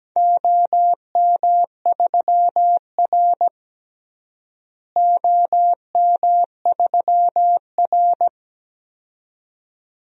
Morse code, heard as a keyed tone, about 17 wpm, 700 Hz. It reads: OM3R OM3R
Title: DXCC-100 Call Signs - 4 Characters - Repeated 2x - 17wpm